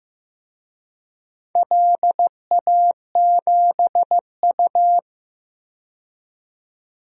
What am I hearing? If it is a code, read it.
Morse code, heard as a keyed tone, about 15 wpm, 700 Hz. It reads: LA7U
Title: DXCC-100 Call Signs - 4 Characters - 15wpm